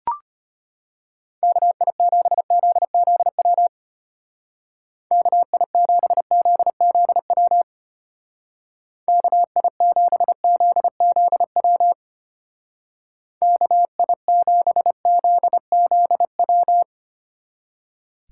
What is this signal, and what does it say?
Morse code, heard as a keyed tone, about 31 wpm, 700 Hz. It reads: KI7ZZW KI7ZZW KI7ZZW KI7ZZW